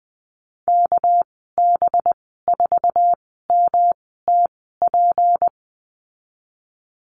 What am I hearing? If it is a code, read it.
Morse code, heard as a keyed tone, about 20 wpm, spaced as for 15 wpm, 700 Hz. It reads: KB4MTP
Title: 207 Lesson 33g - Letter j - Call Signs 20wpm